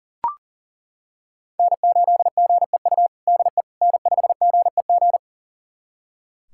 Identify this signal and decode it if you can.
Morse code, heard as a keyed tone, about 40 wpm, 700 Hz. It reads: N8GEU DE N5GEG
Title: USA Callsign DE Callsign - 5 Characters - 40wpm